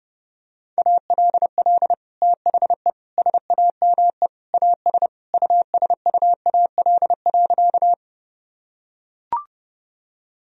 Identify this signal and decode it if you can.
Morse code, heard as a keyed tone, about 30 wpm, 700 Hz. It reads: ALL THE SAME AS USUAL.
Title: Sentences from Top 1000 Words - Review Part 2 - 30wpm